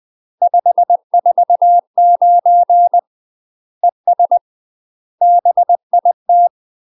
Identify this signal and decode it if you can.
Morse code, heard as a keyed tone, about 20 wpm, 700 Hz. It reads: 549 ES BIT
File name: UK MASTER QSOs TRAINING / 20WPM / UK MASTER QSOs 0001